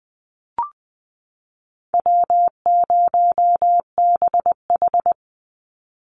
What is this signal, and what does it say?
Morse code, heard as a keyed tone, about 20 wpm, 700 Hz. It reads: W0BH
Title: Call Signs - 20wpm